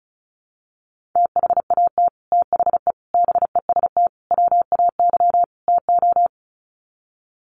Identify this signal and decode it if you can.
Morse code, heard as a keyed tone, about 34 wpm, 700 Hz. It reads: THAT THE BEST WAY TO
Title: Sets of 5 Words Encore2 35wpm